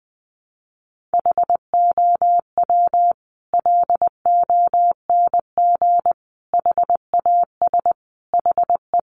Morse code, heard as a keyed tone, about 20 wpm, 700 Hz. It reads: HOW LONG HAS HE
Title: Sentences from Top 100 Words 20wpm